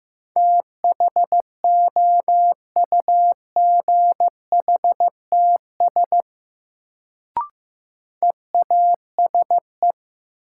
Morse code, heard as a keyed tone, about 15 wpm, 700 Hz. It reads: THOUGHTS EASE
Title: Sentences from Top 1000 Words - Review Part 1 - New Word Drill - 15wpm